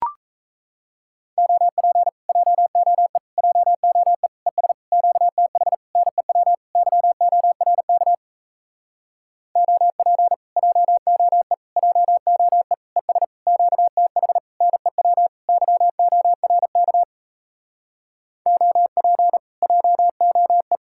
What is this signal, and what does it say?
Morse code, heard as a keyed tone, about 38 wpm, 700 Hz. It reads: OP JOE JOE ES QTH NEW YORK OP JOE JOE ES QTH NEW YORK OP JOE